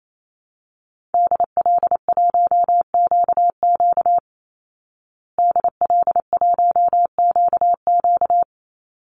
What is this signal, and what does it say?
Morse code, heard as a keyed tone, about 28 wpm, 700 Hz. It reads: DL1QQ DL1QQ
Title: Call Signs Repeated 3x - 28wpm